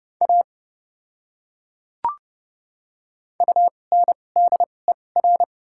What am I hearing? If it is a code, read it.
Morse code, heard as a keyed tone, about 30 wpm, spaced as for 22 wpm, 700 Hz. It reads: A UNDER
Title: 2x Character Spacing - Top 200 Words - 30wpm